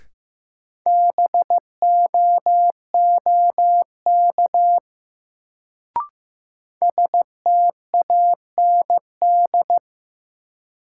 Morse code, heard as a keyed tone, about 15 wpm, 700 Hz. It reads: BOOK STAND
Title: Sentences from Top 300 Words - New Word Drill - 15wpm